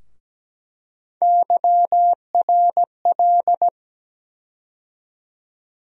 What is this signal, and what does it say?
Morse code, heard as a keyed tone, about 17 wpm, 700 Hz. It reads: YRL
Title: USA Call Sign Postfixes - 3 Letters 17wpm